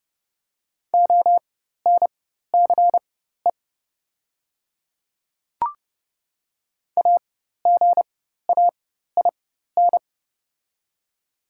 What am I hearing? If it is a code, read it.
Morse code, heard as a keyed tone, about 30 wpm, spaced as for 14 wpm, 700 Hz. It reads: ONCE AGAIN
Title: 4x Character Spacing - Top 300 Words - 30wpm